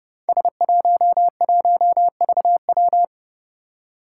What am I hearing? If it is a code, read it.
Morse code, heard as a keyed tone, about 30 wpm, 700 Hz. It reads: S11VW
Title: Five Challenging Letter-Number Combinations 30wpm